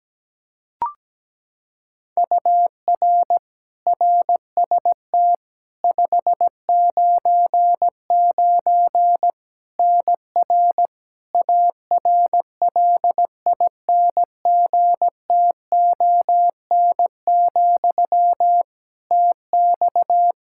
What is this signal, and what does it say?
Morse code, heard as a keyed tone, about 17 wpm, 700 Hz. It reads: UR RST 599 NR ARLINGTON, TX
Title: Practice QSO - Exchange 1 - Beginning - 17wpm